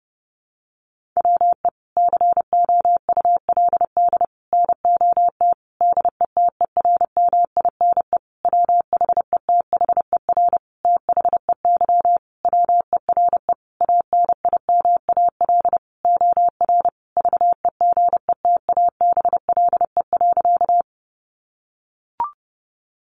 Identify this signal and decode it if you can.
Morse code, heard as a keyed tone, about 28 wpm, 700 Hz. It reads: WE COULD NOT DETERMINE WHETHER THEY WERE ANIMAL OR VEGETABLE.
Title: Sentences from Top 1300 Words 30wpm